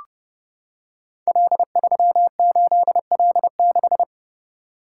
Morse code, heard as a keyed tone, about 30 wpm, 700 Hz. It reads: L38L6
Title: Code Groups - 5 Letter:Numbers - 30wpm